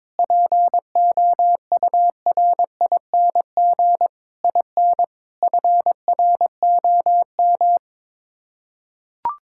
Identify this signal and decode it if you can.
Morse code, heard as a keyed tone, about 21 wpm, 700 Hz. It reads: POURING IN FROM
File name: Sets of 9 Words - 22wpm